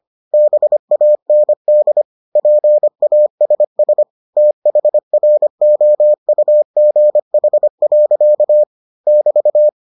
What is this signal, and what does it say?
Morse code, heard as a keyed tone, about 25 wpm, 600 Hz. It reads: BAND PASS THROUGH. <BT>